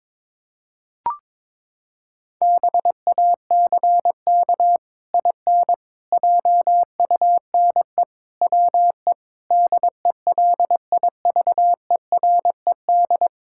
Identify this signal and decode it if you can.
Morse code, heard as a keyed tone, about 22 wpm, 700 Hz. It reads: BACK IN JUNE WE DELIVERED
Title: Pangram Challenge v2 - 22wpm